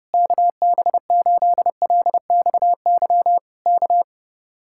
Morse code, heard as a keyed tone, about 30 wpm, 700 Hz. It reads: KB8LXY K